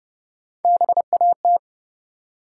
Morse code, heard as a keyed tone, about 30 wpm, 700 Hz. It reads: BAT
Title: Three Letter Words - 30wpm